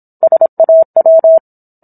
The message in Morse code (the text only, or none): SAW